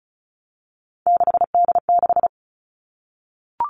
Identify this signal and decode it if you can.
Morse code, heard as a keyed tone, about 36 wpm, 700 Hz. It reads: 6D6
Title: Three Challenging Letter-Number Combinations 35wpm